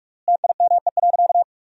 Morse code, heard as a keyed tone, about 46 wpm, 700 Hz. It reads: TIME.
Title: 3x Word-Spacing - Sentences from Top 300 Words - 45wpm